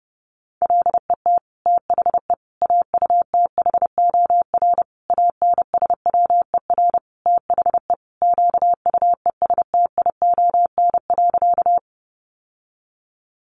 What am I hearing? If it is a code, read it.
Morse code, heard as a keyed tone, about 30 wpm, 700 Hz. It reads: LET THE AUTHOR ANSWER THE QUESTION.
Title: Sentences from Top 1200 Words 30wpm